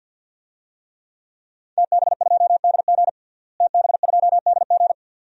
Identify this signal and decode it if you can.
Morse code, heard as a keyed tone, about 52 wpm, 700 Hz. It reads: TB2DG TB2DG
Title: DXCC-100 Call Signs - 5 Characters - Repeated 2x - 50wpm